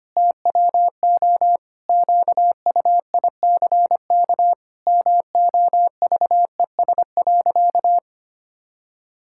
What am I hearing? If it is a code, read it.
Morse code, heard as a keyed tone, about 25 wpm, 700 Hz. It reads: TWO QUICK MOVES.
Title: Pangram Challenge v2 - 25wpm